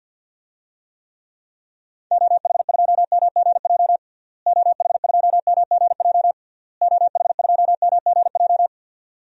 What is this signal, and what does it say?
Morse code, heard as a keyed tone, about 51 wpm, 700 Hz. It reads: OH2MGJ OH2MGJ OH2MGJ
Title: DXCC-100 Call Signs - 6 Characters - Repeated 3x - 50wpm